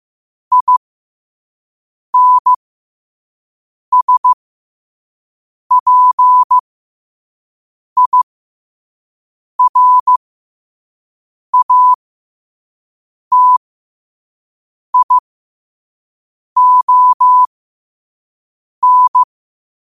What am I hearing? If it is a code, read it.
Morse code, heard as a keyed tone, about 15 wpm, spaced as for 5 wpm, 1000 Hz. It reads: INSPIRATION
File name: qotdfiles / 5wpm